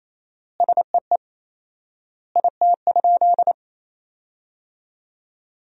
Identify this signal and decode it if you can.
Morse code, heard as a keyed tone, about 28 wpm, 700 Hz. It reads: SEE IT?
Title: 4x Word-Spacing - Sentences from Top 100 Words - 28wpm